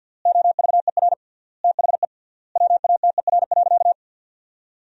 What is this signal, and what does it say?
Morse code, heard as a keyed tone, about 52 wpm, 700 Hz. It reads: OVER THE WATER.
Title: 3x Word-Spacing - Sentences from Top 300 Words - 50wpm